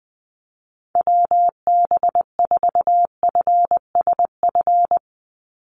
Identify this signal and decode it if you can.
Morse code, heard as a keyed tone, about 19 wpm, 700 Hz. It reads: WB4FSF
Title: Call Signs - 20wpm